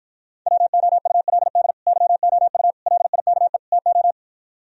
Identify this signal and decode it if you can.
Morse code, heard as a keyed tone, about 55 wpm, 700 Hz. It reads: WOULD YOU LIKE TO